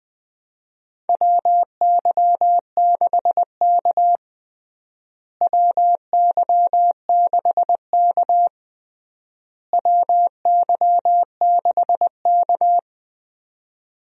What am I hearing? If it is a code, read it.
Morse code, heard as a keyed tone, about 20 wpm, 700 Hz. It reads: WY6K WY6K WY6K
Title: USA Call Signs - 4 Characters - Repeated 3x - 20wpm